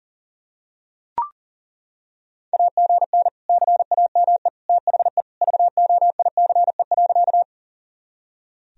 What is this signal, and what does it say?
Morse code, heard as a keyed tone, about 37 wpm, 700 Hz. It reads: AGN CAME THE VOICE.
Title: Sentences from Top 500 Words and Abbreviations 40wpm